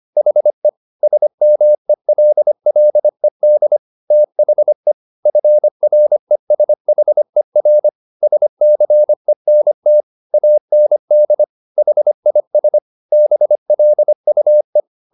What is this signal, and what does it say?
Morse code, heard as a keyed tone, about 25 wpm, 600 Hz. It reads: HE SMELLED THE FRESHER SCENT AND HIS BLUE